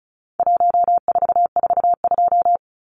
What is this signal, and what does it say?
Morse code, heard as a keyed tone, about 34 wpm, 700 Hz. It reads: 1442
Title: Numbers - 4-Digit 35wpm